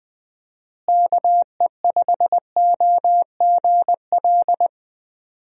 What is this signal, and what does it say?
Morse code, heard as a keyed tone, about 20 wpm, 700 Hz. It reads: KE5OGL
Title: USA Call Signs - 6 Characters - Repeated 2x - 20wpm